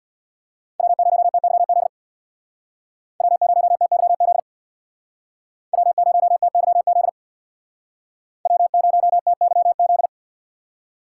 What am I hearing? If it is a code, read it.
Morse code, heard as a keyed tone, about 64 wpm, 700 Hz. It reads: W0TYZ W0TYZ W0TYZ W0TYZ